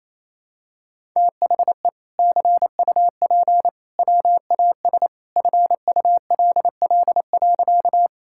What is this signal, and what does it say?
Morse code, heard as a keyed tone, about 28 wpm, 700 Hz. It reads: THE CUP WAS FULL.
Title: Sentences from Top 1600 Words 28wpm